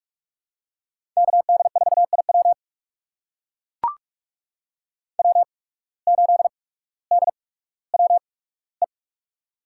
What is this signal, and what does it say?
Morse code, heard as a keyed tone, about 44 wpm, spaced as for 19 wpm, 700 Hz. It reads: KD4IW W8DWE
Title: USA Call Signs - 5 Characters - 8x Character Spacing - 45wpm